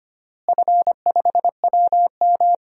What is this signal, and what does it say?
Morse code, heard as a keyed tone, about 25 wpm, 700 Hz. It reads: F5WM
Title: DXCC-100 Call Signs - 4 Characters - Repeated 3x - 25wpm